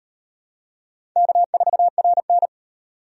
Morse code, heard as a keyed tone, about 33 wpm, 700 Hz. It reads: K4RN